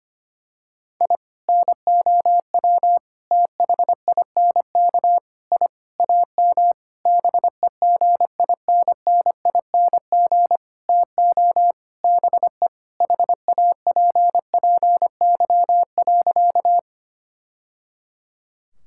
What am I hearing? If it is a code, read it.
Morse code, heard as a keyed tone, about 25 wpm, 700 Hz. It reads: I NOW THINK I AM BEGINNING TO BE HAPPY.
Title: Sentences from Top 1000 Words - Review Part 5 - 25wpm